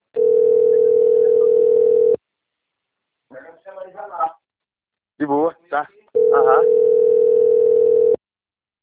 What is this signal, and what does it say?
Ringback tone (North America), 440 + 480 Hz, with speech.
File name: rg-1600-unknown- - - .29863